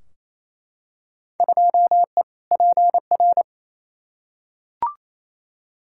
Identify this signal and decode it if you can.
Morse code, heard as a keyed tone, about 28 wpm, 700 Hz. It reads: 2E PR
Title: Field Day Exchanges Encore - 28wpm